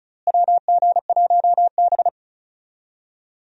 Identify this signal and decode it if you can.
Morse code, heard as a keyed tone, about 34 wpm, 700 Hz. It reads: WG1B
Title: USA Call Signs - 4 Characters - Repeated 2x - 35wpm